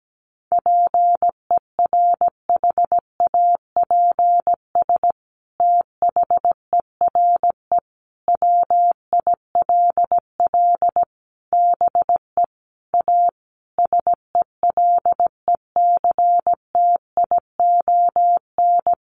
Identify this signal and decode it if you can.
Morse code, heard as a keyed tone, about 17 wpm, 700 Hz. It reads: PERHAPS THERE WILL BE A SELECTION
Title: Sentences from Top 1400 Words 17wpm